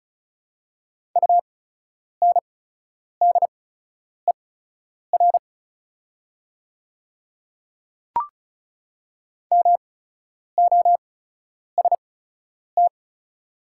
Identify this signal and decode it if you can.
Morse code, heard as a keyed tone, about 36 wpm, spaced as for 10 wpm, 700 Hz. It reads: UNDER MOST
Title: 8x Character Spacing - Top 300 Words - 35wpm